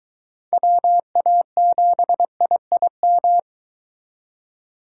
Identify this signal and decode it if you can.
Morse code, heard as a keyed tone, about 22 wpm, 700 Hz. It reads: WA7IIM